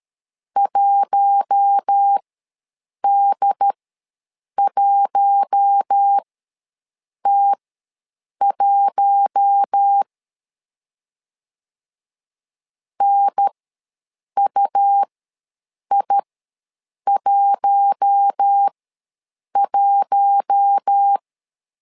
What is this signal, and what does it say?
Morse code, heard as a keyed tone, about 13 wpm, spaced as for 7 wpm, 800 Hz. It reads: 1D1T1 NUI11
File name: LEZIONI / 4 Q1 ws 12wpm 2129